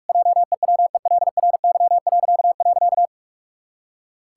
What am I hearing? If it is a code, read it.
Morse code, heard as a keyed tone, about 46 wpm, 700 Hz. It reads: JEWELRY..
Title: Pangram Challenge v2 - 45wpm